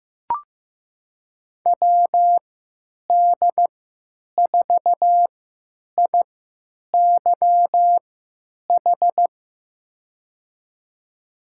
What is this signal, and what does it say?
Morse code, heard as a keyed tone, about 15 wpm, spaced as for 9 wpm, 700 Hz. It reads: WD4IYH
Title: USA Call Signs - 3x Character Spacing - 15wpm